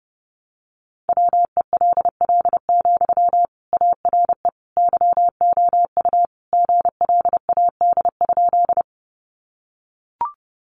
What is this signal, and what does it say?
Morse code, heard as a keyed tone, about 30 wpm, 700 Hz. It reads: WELL, ARE YOU GLAD?
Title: Sentences from Top 800 Words 30wpm